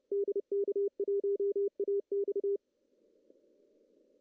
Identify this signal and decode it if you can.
Morse code, heard as a keyed tone, about 30 wpm, 410 Hz. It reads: DK1AX
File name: CW / cqww-cw-k5tr